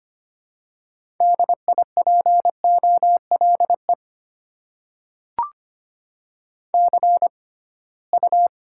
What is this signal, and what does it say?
Morse code, heard as a keyed tone, about 25 wpm, spaced as for 19 wpm, 700 Hz. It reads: DIPOLE CU
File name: Top 90 QSO Elements - 6x Character Spacing - 25wpm